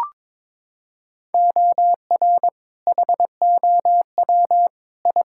Morse code, heard as a keyed tone, about 22 wpm, 700 Hz. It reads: OR HOW I
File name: practice / 22wpm / Sets of 8 Words - 22wpm